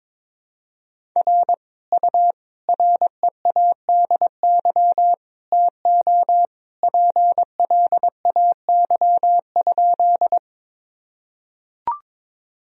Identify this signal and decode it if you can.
Morse code, heard as a keyed tone, about 21 wpm, 700 Hz. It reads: R U READY TO PLAY?
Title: Sentences from Top 400 Words and Abbreviations 22wpm